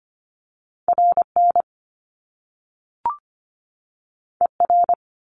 Morse code, heard as a keyed tone, about 25 wpm, 700 Hz. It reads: RN ER
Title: Top Two Letter Combinations in Words 25wpm